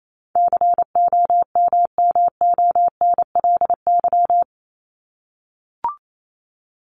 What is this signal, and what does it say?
Morse code, heard as a keyed tone, about 28 wpm, 700 Hz. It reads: COMMONLY
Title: LY Suffix - Top 100 Words with Base Words 28wpm